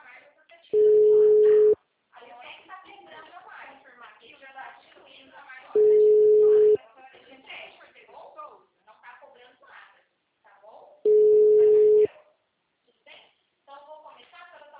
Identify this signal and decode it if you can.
Ringback tone (Europe, 425 Hz), with speech.